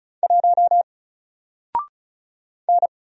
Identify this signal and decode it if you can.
Morse code, heard as a keyed tone, about 35 wpm, 700 Hz. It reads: 1 N
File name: Single-Letter-Number-35wpm